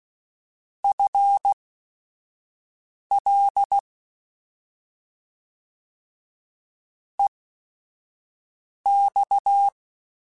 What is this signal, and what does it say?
Morse code, heard as a keyed tone, about 16 wpm, spaced as for 5 wpm, 800 Hz. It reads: FL EX
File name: FL / FL mp3s / Fast FL @5